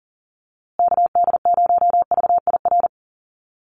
Morse code, heard as a keyed tone, about 40 wpm, 700 Hz. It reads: KD0VIR
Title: USA Call Signs Repeated 3x 40wpm